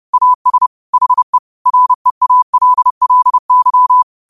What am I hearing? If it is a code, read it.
Morse code, heard as a keyed tone, about 30 wpm, 1000 Hz. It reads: AS HE REALLY
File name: qotdfiles / 30wpm